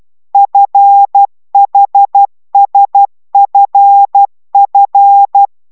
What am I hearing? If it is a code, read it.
Morse code, heard as a keyed tone, about 12 wpm, 800 Hz. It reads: FHSFF
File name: taviro tanulas / cw 69 10 10